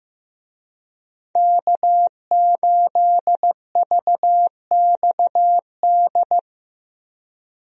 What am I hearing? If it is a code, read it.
Morse code, heard as a keyed tone, about 15 wpm, 700 Hz. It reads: K8VXD
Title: USA Call Signs - 5 Characters - 15wpm